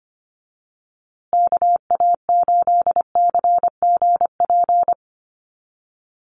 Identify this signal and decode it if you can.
Morse code, heard as a keyed tone, about 25 wpm, 700 Hz. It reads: KA8CGP
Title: USA Call Signs Repeated 2x 25wpm